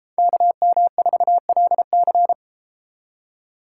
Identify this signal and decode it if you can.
Morse code, heard as a keyed tone, about 33 wpm, 700 Hz. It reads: KM4LC